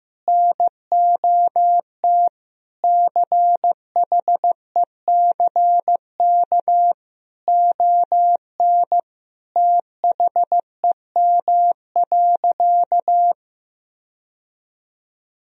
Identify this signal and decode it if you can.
Morse code, heard as a keyed tone, about 15 wpm, 700 Hz. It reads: NOT CHECK ON THEM.
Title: Sentences from Top 1000 Words - Review Part 3 - 15wpm